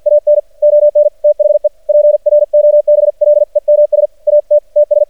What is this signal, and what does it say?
Morse code, heard as a keyed tone, about 50 wpm, 590 Hz. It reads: AN ON THE PROBLEMS AT TS
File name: archived / CRSnet